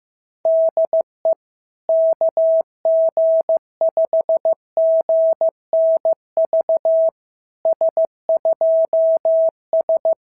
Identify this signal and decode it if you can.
Morse code, heard as a keyed tone, about 15 wpm, 650 Hz. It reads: DE KG5GNV S2S
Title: SOTA QSO Practice 15wpm